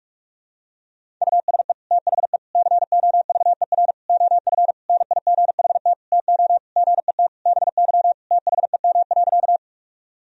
Morse code, heard as a keyed tone, about 46 wpm, 700 Hz. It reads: USE THE COVER OF NIGHT TO GET BY THEM.
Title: Sentences from Top 1000 Words - Review Part 1 - 45wpm